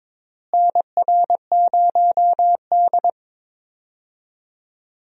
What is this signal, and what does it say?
Morse code, heard as a keyed tone, about 22 wpm, 700 Hz. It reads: NR0D
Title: USA Call Signs - 4 Characters - Repeated 3x - 22wpm